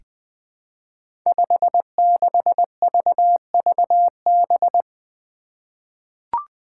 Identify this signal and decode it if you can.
Morse code, heard as a keyed tone, about 20 wpm, 700 Hz. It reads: 56VVB
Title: Five Challenging Letter-Number Combinations 20wpm